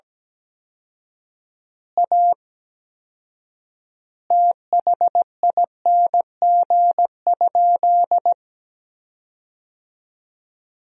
Morse code, heard as a keyed tone, about 17 wpm, 700 Hz. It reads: A THING?
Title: 4x Word-Spacing - Sentences from Top 200 Words - 17wpm